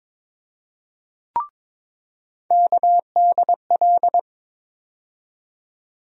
Morse code, heard as a keyed tone, about 22 wpm, 700 Hz. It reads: KDL
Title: USA Call Sign Postfixes - 3 Letters 22wpm